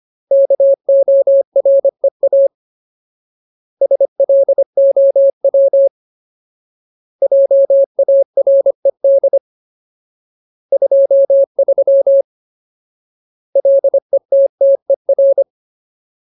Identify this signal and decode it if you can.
Morse code, heard as a keyed tone, about 25 wpm, 550 Hz. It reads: KOREA SLOW JARED 23 LETTER
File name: Ultimate Word List / 25WPM Ultimate